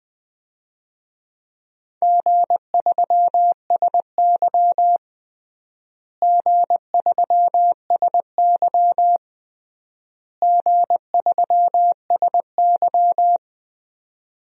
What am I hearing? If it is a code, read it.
Morse code, heard as a keyed tone, about 20 wpm, 700 Hz. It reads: G3SY G3SY G3SY
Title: DXCC-100 Call Signs - 4 Characters - Repeated 3x - 20wpm